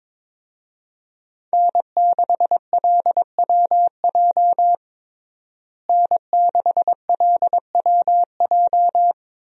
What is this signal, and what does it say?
Morse code, heard as a keyed tone, about 22 wpm, 700 Hz. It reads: N6LWJ N6LWJ
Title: USA Call Signs - 5 Characters - Repeated 3x - 22wpm